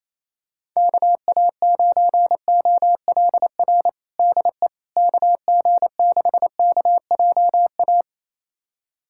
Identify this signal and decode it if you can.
Morse code, heard as a keyed tone, about 28 wpm, 700 Hz. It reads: KA9OLR DE KG6KJA
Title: USA Callsign DE Callsign - 6 Characters - 28wpm